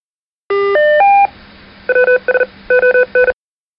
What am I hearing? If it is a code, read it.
Morse code, heard as a keyed tone, about 40 wpm, 500 Hz. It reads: WS ON
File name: rpt-ws-on